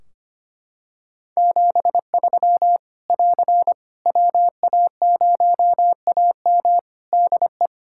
Morse code, heard as a keyed tone, about 25 wpm, 700 Hz. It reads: 73 <AR> WA0AM DE
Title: K1USN SST QSO Practice 25wpm